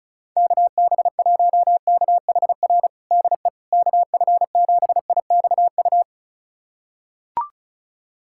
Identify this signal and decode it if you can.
Morse code, heard as a keyed tone, about 36 wpm, 700 Hz. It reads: KB1KHR DE KF7IXU
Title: USA Callsign DE Callsign - 6 Characters - 35wpm